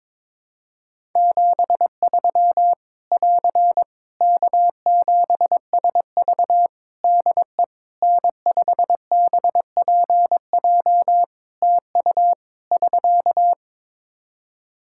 Morse code, heard as a keyed tone, about 22 wpm, 700 Hz. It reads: 73 <AR> K7SV DE N5BPJ TU <SK>
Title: Practice QSO - Exchange 4 - Ending - 22wpm